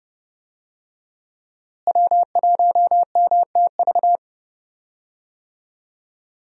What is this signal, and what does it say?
Morse code, heard as a keyed tone, about 30 wpm, 700 Hz. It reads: W1MTV
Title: USA Call Signs - 5 Characters - Repeated 3x - 30wpm